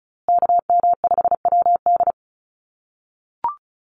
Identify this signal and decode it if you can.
Morse code, heard as a keyed tone, about 34 wpm, 700 Hz. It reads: KM5WD
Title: USA Call Signs Repeated 2x 35wpm